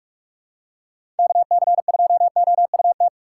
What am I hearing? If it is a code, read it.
Morse code, heard as a keyed tone, about 42 wpm, 690 Hz. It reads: KC2OUT
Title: USA Call Signs - 6 Characters - 3x Character Spacing - 45wpm